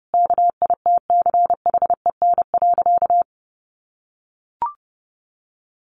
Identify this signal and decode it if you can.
Morse code, heard as a keyed tone, about 30 wpm, 700 Hz. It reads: KITCHEN.
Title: Sentences from Top 1500 Words 30wpm